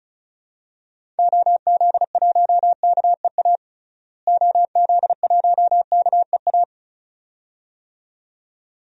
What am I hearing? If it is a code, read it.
Morse code, heard as a keyed tone, about 34 wpm, 700 Hz. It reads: OZ1KEA OZ1KEA
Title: DXCC-100 Call Signs - 6 Characters - Repeated 2x - 35wpm